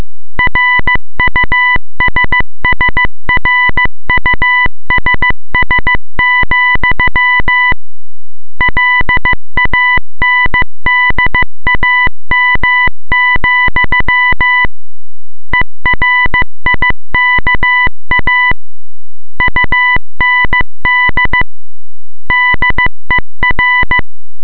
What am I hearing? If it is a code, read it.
Morse code, heard as a keyed tone, about 15 wpm, 2000 Hz, with noise. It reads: RUSSRUSS, LANDAM, ERIKA UND DER